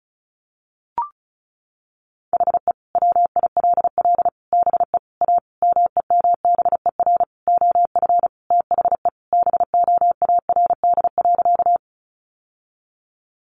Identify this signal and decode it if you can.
Morse code, heard as a keyed tone, about 34 wpm, 700 Hz. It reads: HE WILL BE A MEMBER OF THE BOARD.
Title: Sentences from Top 1100 Words 35wpm